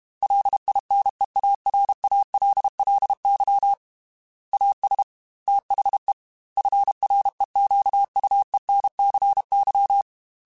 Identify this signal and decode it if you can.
Morse code, heard as a keyed tone, about 32 wpm, 800 Hz. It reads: LINEARALLY AS THE FREQUENCY
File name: Technical Subjects / Technical Subjects-mp3s / XL @30